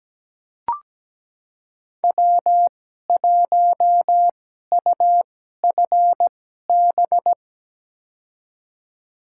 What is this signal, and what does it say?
Morse code, heard as a keyed tone, about 17 wpm, spaced as for 12 wpm, 700 Hz. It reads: W1UFB
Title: USA Call Signs - 5 Characters - 2x Character Spacing - 17wpm